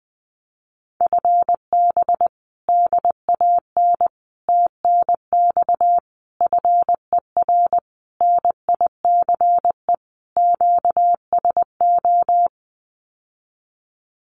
Morse code, heard as a keyed tone, about 20 wpm, 700 Hz. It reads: FB DAN TNX FER NICE QSO
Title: Common QSO Phrases Encore 20wpm